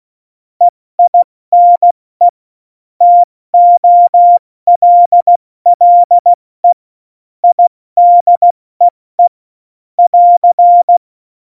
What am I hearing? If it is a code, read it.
Morse code, heard as a keyed tone, about 16 wpm, 700 Hz. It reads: EINE TOLLE IDEE <AR>
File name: Morsekurs / Witze / Witz 20. Oktober 2020, 70 F80